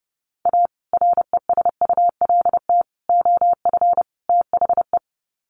Morse code, heard as a keyed tone, about 30 wpm, 700 Hz. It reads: A RESULT OF THE
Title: Sets of 4 Words 30wpm